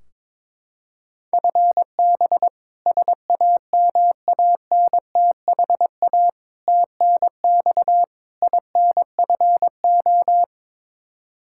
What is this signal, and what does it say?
Morse code, heard as a keyed tone, about 22 wpm, 700 Hz. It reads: FB SAMANTHA TNX INFO
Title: K1USN SST QSO Practice 22wpm